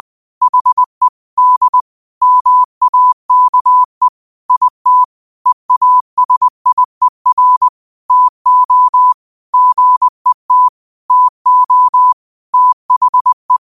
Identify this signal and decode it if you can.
Morse code, heard as a keyed tone, about 20 wpm, 1000 Hz. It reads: HE D MAKE IT EASIER TO GET TO THE
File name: qotdfiles / 20wpm